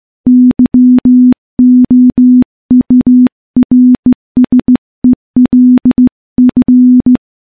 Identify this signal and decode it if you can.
Morse code, heard as a keyed tone, about 16 wpm, 250 Hz. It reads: YOURSELF
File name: morse-training-15WPM